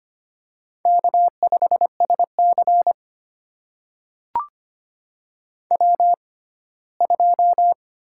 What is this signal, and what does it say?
Morse code, heard as a keyed tone, about 25 wpm, spaced as for 17 wpm, 700 Hz. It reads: K5SC W2
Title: USA Call Signs - 4 Characters - 6x Character Spacing - 25wpm